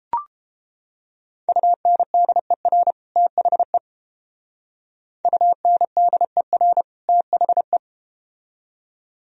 Morse code, heard as a keyed tone, about 30 wpm, 700 Hz. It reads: UNDER THE UNDER THE